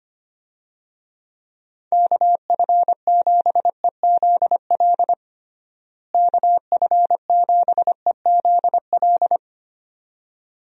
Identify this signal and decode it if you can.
Morse code, heard as a keyed tone, about 25 wpm, 700 Hz. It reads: KF7EZL KF7EZL
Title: USA Call Signs - 6 Characters - Repeated 2x - 25wpm